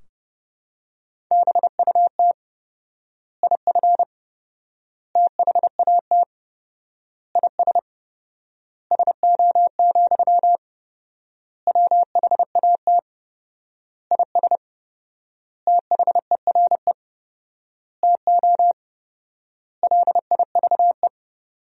Morse code, heard as a keyed tone, about 32 wpm, 700 Hz. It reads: BUT IF THAT IS SO, WHAT IS THERE TO LIVE
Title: 4x Word-Spacing - Sentences from Top 200 Words - 30wpm